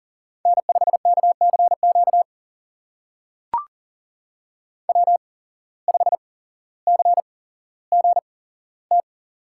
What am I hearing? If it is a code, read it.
Morse code, heard as a keyed tone, about 40 wpm, spaced as for 17 wpm, 700 Hz. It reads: N5KCQ W5CGT
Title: USA Call Signs - 5 Characters - 8x Character Spacing - 40wpm